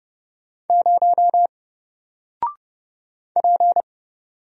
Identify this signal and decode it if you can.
Morse code, heard as a keyed tone, about 30 wpm, 700 Hz. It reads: 0 P
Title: Single Letter-Number 30wpm